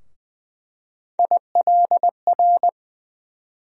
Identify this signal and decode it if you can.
Morse code, heard as a keyed tone, about 20 wpm, 700 Hz. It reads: ILR
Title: USA Call Sign Postfixes - 3 Letters 20wpm